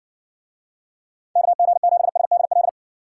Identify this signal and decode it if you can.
Morse code, heard as a keyed tone, about 53 wpm, 700 Hz. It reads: KG7SDL